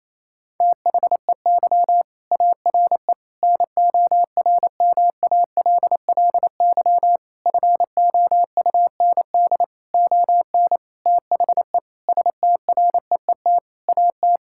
Morse code, heard as a keyed tone, about 28 wpm, 700 Hz. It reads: THEY ARE NORMALLY FOUND ON THE STREET AT
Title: Sentences from Top 1000 Words - Review Part 6 - 28wpm